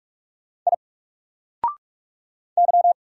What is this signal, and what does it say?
Morse code, heard as a keyed tone, about 44 wpm, 700 Hz. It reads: I Y
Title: Single Character (Letter:Number:Punctuation:Prosign) 45wpm